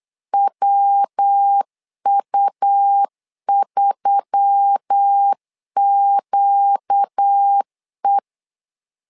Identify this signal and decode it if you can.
Morse code, heard as a keyed tone, about 8 wpm, 800 Hz. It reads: WU3QE
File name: LEZIONI / 4 Q1S3 8wpm 3037